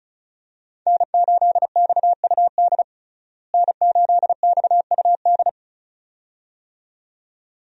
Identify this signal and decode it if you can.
Morse code, heard as a keyed tone, about 36 wpm, 700 Hz. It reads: N8XUD N8XUD
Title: USA Call Signs - 5 Characters - Repeated 2x - 35wpm